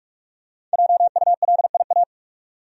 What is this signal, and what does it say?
Morse code, heard as a keyed tone, about 45 wpm, 700 Hz. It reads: JULIA
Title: USA Common Names - 45wpm